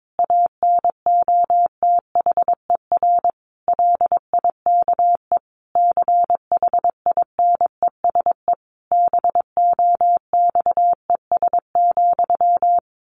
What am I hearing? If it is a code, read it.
Morse code, heard as a keyed tone, about 22 wpm, 700 Hz. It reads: ANOTHER LIKE CHINESE BOXES,
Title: A Sound Of Thunder 22wpm